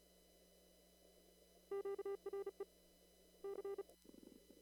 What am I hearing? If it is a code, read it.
Morse code, heard as a keyed tone, about 36 wpm, 400 Hz. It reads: QRE C
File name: CW / kt5j wpx cw-op-k5pi